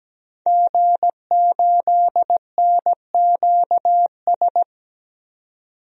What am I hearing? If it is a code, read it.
Morse code, heard as a keyed tone, about 17 wpm, 700 Hz. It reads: G8NQS
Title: DXCC-100 Call Signs - 5 Characters - 17wpm